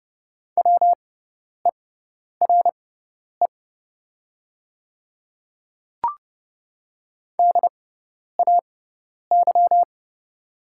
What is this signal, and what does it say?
Morse code, heard as a keyed tone, about 29 wpm, spaced as for 10 wpm, 700 Hz. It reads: WERE DAY
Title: 6x Character Spacing - Top 100 Words - 30wpm